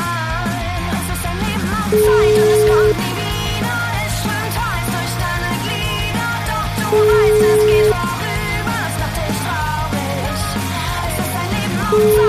Ringback tone (Europe, 425 Hz), with noise.